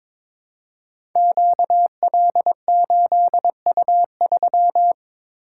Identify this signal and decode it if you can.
Morse code, heard as a keyed tone, about 22 wpm, 700 Hz. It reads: QL8U3
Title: Code Groups - 5 Letter:Numbers - 22wpm